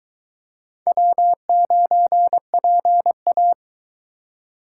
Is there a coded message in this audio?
Yes: Morse code from 0.9 to 3.5 s, heard as a keyed tone, about 22 wpm, 700 Hz. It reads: W9PA